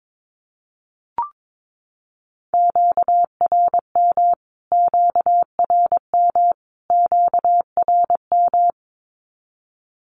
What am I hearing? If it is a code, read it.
Morse code, heard as a keyed tone, about 22 wpm, 700 Hz. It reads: QRM QRM QRM
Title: Q-Codes Repeated 3x 22wpm